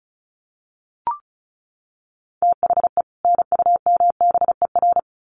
Morse code, heard as a keyed tone, about 34 wpm, 700 Hz. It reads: THE NUMBER
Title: Sets of 2 Words 35wpm